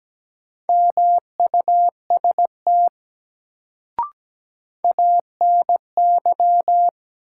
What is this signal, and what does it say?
Morse code, heard as a keyed tone, about 17 wpm, 700 Hz. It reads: MUST ANY
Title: Sentences from Top 200 Words - New Word Drill - 17wpm